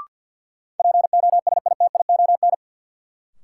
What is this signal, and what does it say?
Morse code, heard as a keyed tone, about 52 wpm, 700 Hz. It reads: POSITION
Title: Sentences from Top 900 Words - New Word Drill - 50wpm